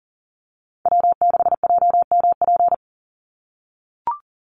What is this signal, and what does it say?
Morse code, heard as a keyed tone, about 41 wpm, 700 Hz. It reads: W6JMP
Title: NA SOTA Chaser Callsigns 40wpm